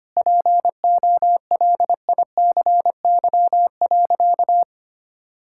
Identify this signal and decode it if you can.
Morse code, heard as a keyed tone, about 25 wpm, 700 Hz. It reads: POLICY.
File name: practice / 25wpm / Difficult Sentences with Hints - 25wpm